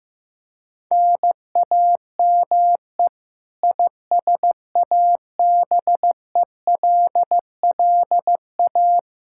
Morse code, heard as a keyed tone, about 15 wpm, 700 Hz. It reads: NAME ISABELLA
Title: K1USN SST QSO Practice 15wpm